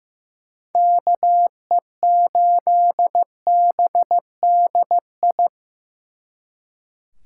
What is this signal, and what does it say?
Morse code, heard as a keyed tone, about 18 wpm, 700 Hz. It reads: KE8BDI
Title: Speed Racing - USA Call Signs 15wpm